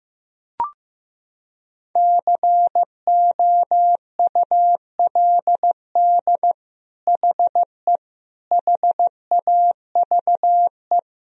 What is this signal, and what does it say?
Morse code, heard as a keyed tone, about 15 wpm, 700 Hz. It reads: COULD HE HAVE
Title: Sentences from Top 1000 Words - Review Part 3 - 15wpm